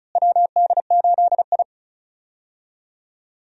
Morse code, heard as a keyed tone, about 36 wpm, 700 Hz. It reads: WD8I
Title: USA Call Signs - 4 Characters - Repeated 2x - 35wpm